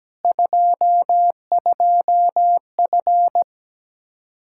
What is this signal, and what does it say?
Morse code, heard as a keyed tone, about 17 wpm, 700 Hz. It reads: 22F
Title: Practice QSO - Exchange 2 - Info - 17wpm